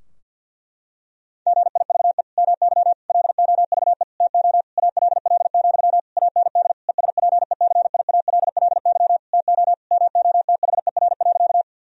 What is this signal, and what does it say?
Morse code, heard as a keyed tone, about 50 wpm, 700 Hz. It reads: GIVE MY LOVE TO ALL, AND ESPECIALLY TO MOTHER.
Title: Sentences from Top 1000 Words - Review Part 4 - 50wpm